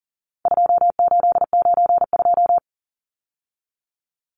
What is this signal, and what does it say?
Morse code, heard as a keyed tone, about 40 wpm, 700 Hz. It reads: 2892
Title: Numbers - 4-Digit 40wpm